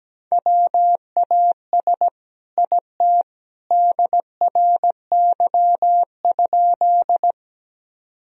Morse code, heard as a keyed tone, about 17 wpm, 700 Hz. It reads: WAS IT DRY?
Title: Sentences from Top 1000 Words - Review Part 3 - 17wpm